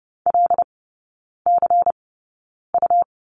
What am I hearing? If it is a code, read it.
Morse code, heard as a keyed tone, about 30 wpm, spaced as for 9 wpm, 700 Hz. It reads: LCU
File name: Lesson 18e - Letter m - Three Characters 30wpm